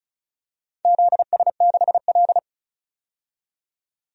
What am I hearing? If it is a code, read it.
Morse code, heard as a keyed tone, about 36 wpm, 700 Hz. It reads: ZS6L
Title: DXCC-100 Call Signs - 4 Characters - 35wpm